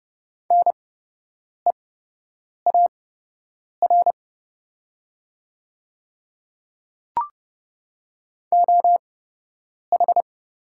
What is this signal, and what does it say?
Morse code, heard as a keyed tone, about 30 wpm, spaced as for 8 wpm, 700 Hz. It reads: NEAR OH
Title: 8x Character Spacing - Top 500 Words - 30wpm